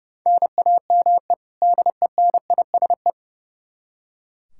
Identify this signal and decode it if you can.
Morse code, heard as a keyed tone, about 32 wpm, 700 Hz. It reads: NAME DENISE
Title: Practice QSO - Exchange 1 - Beginning - 30wpm